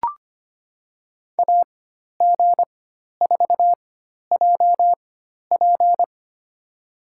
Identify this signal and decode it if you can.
Morse code, heard as a keyed tone, about 25 wpm, spaced as for 12 wpm, 700 Hz. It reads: AG4JP
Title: USA Call Signs - 5 Characters - 4x Character Spacing - 25wpm